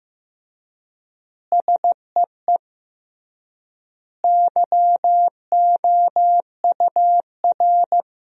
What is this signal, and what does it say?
Morse code, heard as a keyed tone, about 15 wpm, 700 Hz. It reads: SEE YOUR
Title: 3x Word-Spacing - Sentences from Top 500 Words - 15wpm